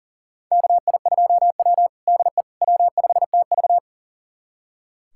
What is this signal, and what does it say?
Morse code, heard as a keyed tone, about 40 wpm, 700 Hz. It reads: KI2W DE W5TV
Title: USA Callsign DE Callsign - 4 Characters - 40wpm